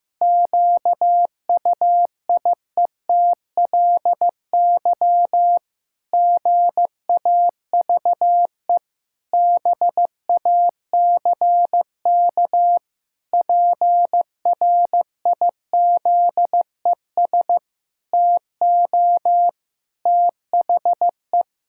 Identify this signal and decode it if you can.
Morse code, heard as a keyed tone, about 15 wpm, 700 Hz. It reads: QUIETLY GAVE BACK PRIZES TO THE
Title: Pangram Challenge v2 - 15wpm